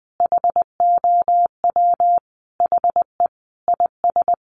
Morse code, heard as a keyed tone, about 20 wpm, 700 Hz. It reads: HOW HE IS
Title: Additional Easy Phrases 20wpm